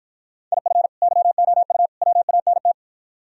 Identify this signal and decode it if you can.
Morse code, heard as a keyed tone, about 53 wpm, 700 Hz. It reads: IF YOU WANT